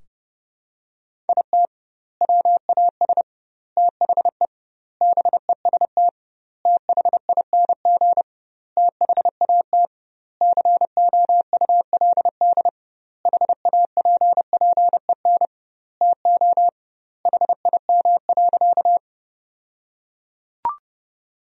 Morse code, heard as a keyed tone, about 30 wpm, 700 Hz. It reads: IT WAS THE BEST THING THAT COULD HAPPEN TO HIM.
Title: 2x Word-Spacing - Sentences from Top 400 Words - 30wpm